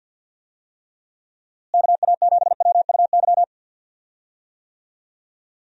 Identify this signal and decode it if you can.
Morse code, heard as a keyed tone, about 51 wpm, 700 Hz. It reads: KA7WUY
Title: USA Call Signs - 6 Characters - Repeated 2x - 50wpm